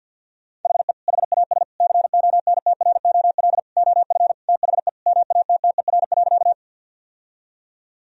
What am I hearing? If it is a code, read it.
Morse code, heard as a keyed tone, about 46 wpm, 700 Hz. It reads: HE HAS CONTROL OF THE MATTER.
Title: Sentences from Top 1000 Words - Review Part 3 - 50wpm